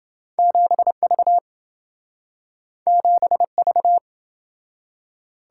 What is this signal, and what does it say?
Morse code, heard as a keyed tone, about 27 wpm, 700 Hz. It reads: 7V 7V